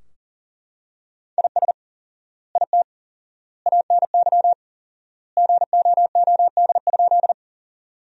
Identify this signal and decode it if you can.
Morse code, heard as a keyed tone, about 40 wpm, 700 Hz. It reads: IS IT ANY GOOD?
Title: 4x Word-Spacing - Sentences from Top 200 Words - 40wpm